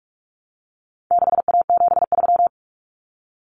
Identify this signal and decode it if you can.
Morse code, heard as a keyed tone, about 44 wpm, 700 Hz. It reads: 6A73
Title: Common Cut Numbers - 4-Digit 45wpm